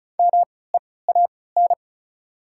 Morse code, heard as a keyed tone, about 36 wpm, spaced as for 20 wpm, 700 Hz. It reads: MEAN
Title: 3x Character Spacing - Top 200 Words - 35wpm